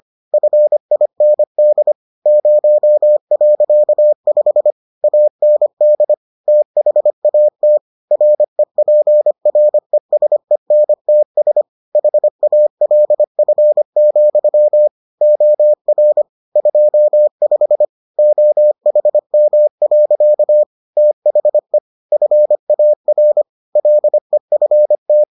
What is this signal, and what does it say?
Morse code, heard as a keyed tone, about 25 wpm, 600 Hz. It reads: FIND 0.5 AND THAT REPRESENTS HALF, OR 25 OHM. THE FAR LEFT